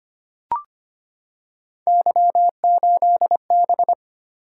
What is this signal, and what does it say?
Morse code, heard as a keyed tone, about 25 wpm, 700 Hz. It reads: Y8B
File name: Three Letter-Number Combinations - 25wpm